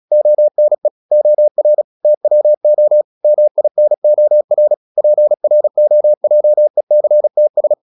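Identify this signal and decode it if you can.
Morse code, heard as a keyed tone, about 36 wpm, 600 Hz. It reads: ONE OR TWO MINOR PROJECTS
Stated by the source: Morse practice at 36 wpm, text of Ken Follett's novel Never, from CW med Gnister.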